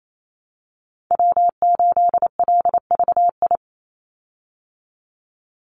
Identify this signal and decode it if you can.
Morse code, heard as a keyed tone, about 28 wpm, 700 Hz. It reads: W8LVI
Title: Call Signs Repeated 2x - 28wpm